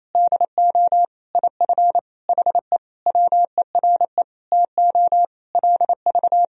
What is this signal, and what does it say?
Morse code, heard as a keyed tone, about 28 wpm, 700 Hz. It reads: DO IF HE WERE TO LV
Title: Sentences from Top 400 Words and Abbreviations 28wpm